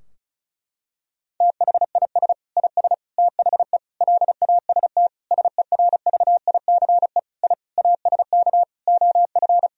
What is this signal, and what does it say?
Morse code, heard as a keyed tone, about 36 wpm, 700 Hz. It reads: THIS IS THE LAST SERVICE I ASK OF
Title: Sentences from Top 1000 Words - Review Part 4 - 35wpm